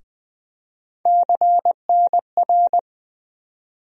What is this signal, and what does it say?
Morse code, heard as a keyed tone, about 20 wpm, 700 Hz. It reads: CNR
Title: USA Call Sign Postfixes - 3 Letters 20wpm